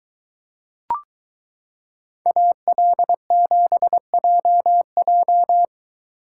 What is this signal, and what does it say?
Morse code, heard as a keyed tone, about 23 wpm, 700 Hz. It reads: AL7JJ